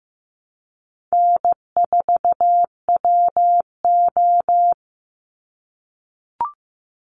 Morse code, heard as a keyed tone, about 15 wpm, 700 Hz. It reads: N4WO
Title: Easy Call Signs - 15wpm